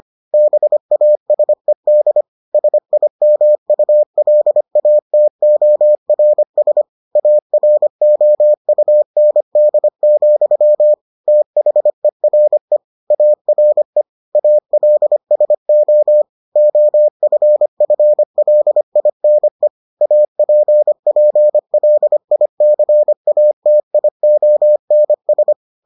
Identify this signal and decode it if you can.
Morse code, heard as a keyed tone, about 25 wpm, 600 Hz. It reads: BASED SIMULATORS AROUND, THERE ARE ALSO OFFLINE APPLICATIONS